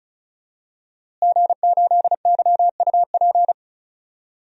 Morse code, heard as a keyed tone, about 34 wpm, 700 Hz. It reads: G8YUP
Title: DXCC-100 Call Signs - 5 Characters - Repeated 2x - 35wpm